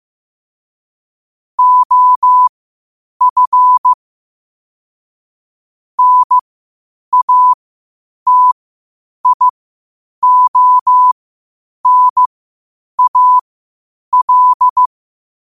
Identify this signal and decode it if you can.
Morse code, heard as a keyed tone, about 15 wpm, spaced as for 8 wpm, 1000 Hz. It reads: OF NATIONAL
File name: qotdfiles / 8wpm